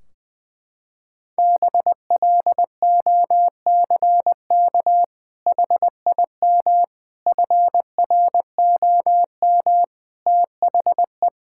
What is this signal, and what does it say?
Morse code, heard as a keyed tone, about 20 wpm, 700 Hz. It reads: BLOCK HIM FROM THE
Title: Sentences from Top 1600 Words 20wpm